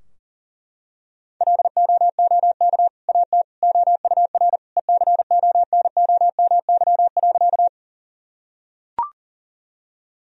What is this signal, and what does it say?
Morse code, heard as a keyed tone, about 40 wpm, 700 Hz. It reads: LOOK AT OUR ECONOMY.
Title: Sentences from Top 1000 Words - Review Part 3 - 40wpm